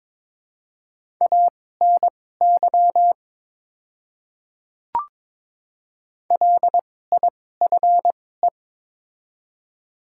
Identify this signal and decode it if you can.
Morse code, heard as a keyed tone, about 22 wpm, spaced as for 16 wpm, 700 Hz. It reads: ANY LIFE
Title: 2x Character Spacing - Top 500 Words - 22wpm